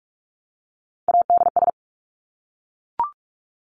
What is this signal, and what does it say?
Morse code, heard as a keyed tone, about 44 wpm, 700 Hz. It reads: ADS
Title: Abbreviations Repeated 3x - 45wpm